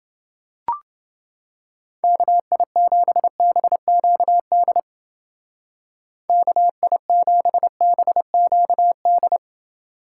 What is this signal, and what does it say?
Morse code, heard as a keyed tone, about 29 wpm, 700 Hz. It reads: KI7BQD KI7BQD